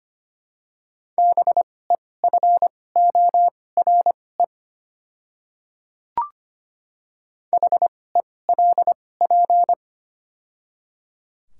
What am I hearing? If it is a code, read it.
Morse code, heard as a keyed tone, about 25 wpm, spaced as for 18 wpm, 690 Hz. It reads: BEFORE HELP
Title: 2x Character Spacing - Top 300 Words - 25wpm